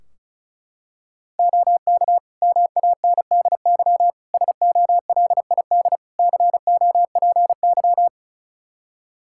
Morse code, heard as a keyed tone, about 36 wpm, 700 Hz. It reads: OK MANDY SOLID COPY
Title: Practice QSO - Exchange 3 - More Info - 35wpm